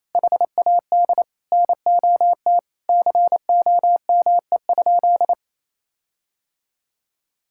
Morse code, heard as a keyed tone, about 28 wpm, 700 Hz. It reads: HAD NOT COME?
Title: Sentences from Top 1600 Words 28wpm